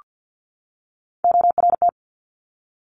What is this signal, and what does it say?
Morse code, heard as a keyed tone, about 50 wpm, 700 Hz. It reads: ORT
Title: Top Three Letter Combinations in Words 50wpm